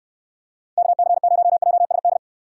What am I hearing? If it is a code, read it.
Morse code, heard as a keyed tone, about 61 wpm, 700 Hz. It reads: KC0JSD